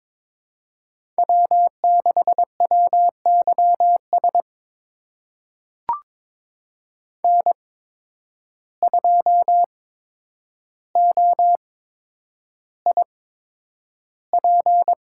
Morse code, heard as a keyed tone, about 22 wpm, spaced as for 9 wpm, 700 Hz. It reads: W6WYS N2OIP
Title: USA Call Signs - 8x Character Spacing - 22wpm